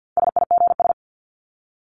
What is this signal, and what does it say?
Morse code, heard as a keyed tone, about 48 wpm, 700 Hz. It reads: SIGS
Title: Abbreviations 50wpm